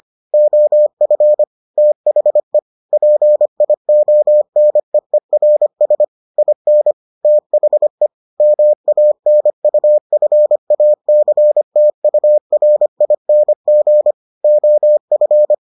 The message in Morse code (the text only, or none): OF THE PIONEERS IN THE MANUFACTURING OF